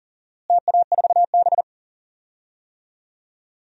Morse code, heard as a keyed tone, about 37 wpm, 700 Hz. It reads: TA4B
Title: DXCC-100 Call Signs - 4 Characters - Repeated 2x - 40wpm